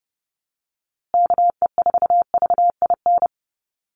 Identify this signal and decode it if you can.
Morse code, heard as a keyed tone, about 30 wpm, 700 Hz. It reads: KE4VIN
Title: USA Call Signs Repeated 3x 30wpm